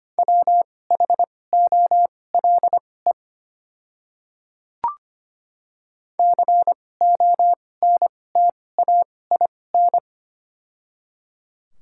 Morse code, heard as a keyed tone, about 25 wpm, spaced as for 18 wpm, 700 Hz. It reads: WHOLE CONTAIN
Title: 2x Character Spacing - Top 500 Words - 25wpm